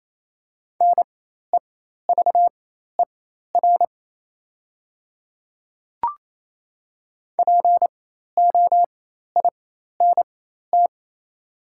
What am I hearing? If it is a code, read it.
Morse code, heard as a keyed tone, about 28 wpm, spaced as for 13 wpm, 700 Hz. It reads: NEVER POINT
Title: 4x Character Spacing - Top 300 Words - 28wpm